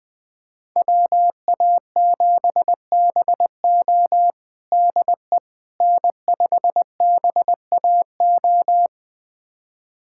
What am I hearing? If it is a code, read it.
Morse code, heard as a keyed tone, about 20 wpm, 700 Hz. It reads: WA7BO DE N5BAO
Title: USA Callsign DE Callsign - 5 Characters - 20wpm